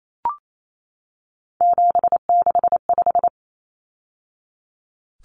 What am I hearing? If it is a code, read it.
Morse code, heard as a keyed tone, about 28 wpm, 700 Hz. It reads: 765
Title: Numbers - 3-Digit 28wpm